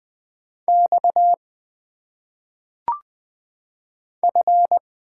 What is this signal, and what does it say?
Morse code, heard as a keyed tone, about 20 wpm, 700 Hz. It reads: X F
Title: Single Letters - Phonetic Pronunciation 20wpm